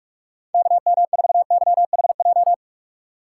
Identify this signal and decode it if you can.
Morse code, heard as a keyed tone, about 46 wpm, 700 Hz. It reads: KM4YHJ
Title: USA Call Signs - 2x Character Spacing - 45wpm